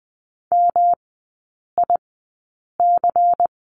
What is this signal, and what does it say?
Morse code, heard as a keyed tone, about 20 wpm, spaced as for 8 wpm, 700 Hz. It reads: MIC
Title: 159 Lesson 26e - Letter g - Three Characters 20wpm